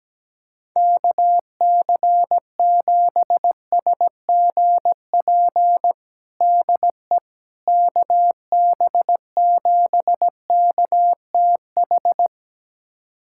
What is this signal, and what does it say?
Morse code, heard as a keyed tone, about 17 wpm, 700 Hz. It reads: KC7SGP DE KB7KTH
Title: USA Callsign DE Callsign 17wpm